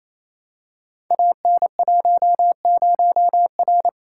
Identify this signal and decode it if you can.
Morse code, heard as a keyed tone, about 27 wpm, 700 Hz. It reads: AN10R
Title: Code Groups - 5 Letter:Numbers - 28wpm